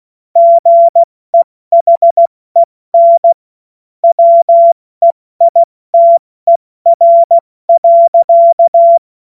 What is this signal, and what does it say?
Morse code, heard as a keyed tone, about 16 wpm, spaced as for 14 wpm, 680 Hz. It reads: GEHEN WEITER.
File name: Morsekurs / Witze / Witz 30. Dezember 2025 70 F80